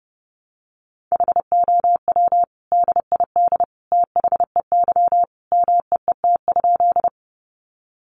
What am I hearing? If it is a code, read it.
Morse code, heard as a keyed tone, about 30 wpm, 700 Hz. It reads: HOW DID THEY MEET?
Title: Sentences from Top 1100 Words 30wpm